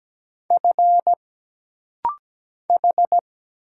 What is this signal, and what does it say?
Morse code, heard as a keyed tone, about 17 wpm, 700 Hz. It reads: F H
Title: Single Letter-Number Alt - Rapid-Fire 17wpm